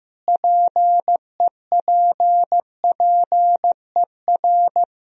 Morse code, heard as a keyed tone, about 15 wpm, 700 Hz. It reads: PEPPER
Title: Speed-Racing - Sets of 5 Words 15wpm